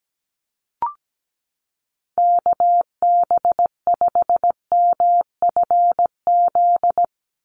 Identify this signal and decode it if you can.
Morse code, heard as a keyed tone, about 17 wpm, 700 Hz. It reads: KB5MFZ
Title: USA Call Signs 17wpm